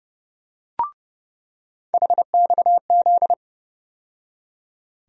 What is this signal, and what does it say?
Morse code, heard as a keyed tone, about 30 wpm, 700 Hz. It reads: HXZ
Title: USA Call Sign Postfixes - 3 Letters 30wpm